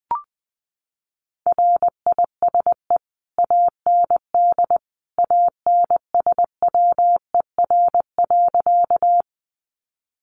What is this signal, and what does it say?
Morse code, heard as a keyed tone, about 20 wpm, 700 Hz. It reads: RISE AND ANSWER.
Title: Sentences from Top 900 Words 20wpm